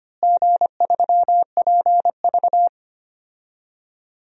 Morse code, heard as a keyed tone, about 25 wpm, 700 Hz. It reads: G3PV
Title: DXCC-100 Call Signs - 4 Characters - Repeated 3x - 25wpm